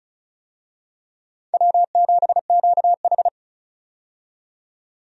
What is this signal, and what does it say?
Morse code, heard as a keyed tone, about 36 wpm, 700 Hz. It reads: W7QH
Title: USA Call Signs - 4 Characters - 35wpm